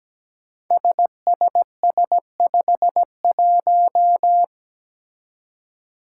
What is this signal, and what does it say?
Morse code, heard as a keyed tone, about 17 wpm, 700 Hz. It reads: SSS51
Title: Five Challenging Letter-Number Combinations 17wpm